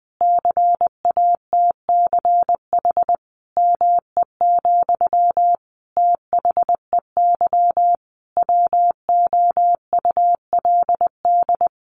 Morse code, heard as a keyed tone, about 20 wpm, 700 Hz. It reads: CATCH ME, THEY WOULD
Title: Sentences from Top 1500 Words 20wpm